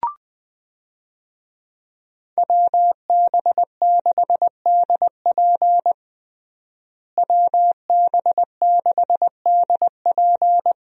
Morse code, heard as a keyed tone, about 20 wpm, 700 Hz. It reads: WB6DP WB6DP
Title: USA Call Signs - 5 Characters - Repeated 2x - 20wpm